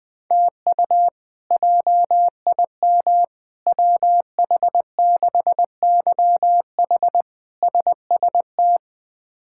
Morse code, heard as a keyed tone, about 20 wpm, 700 Hz. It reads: TU JIM WH6YH SST
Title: K1USN SST QSO Practice - 20wpm